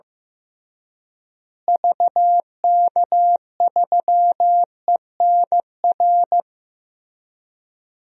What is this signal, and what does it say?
Morse code, heard as a keyed tone, about 16 wpm, 700 Hz. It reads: VK3ENR
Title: DXCC-100 Call Signs - 6 Characters - Repeated 3x - 15wpm